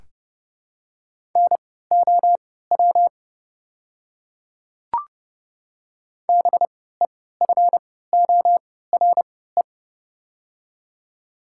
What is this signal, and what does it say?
Morse code, heard as a keyed tone, about 32 wpm, spaced as for 17 wpm, 700 Hz. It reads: NOW BEFORE
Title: 3x Character Spacing - Top 200 Words - 30wpm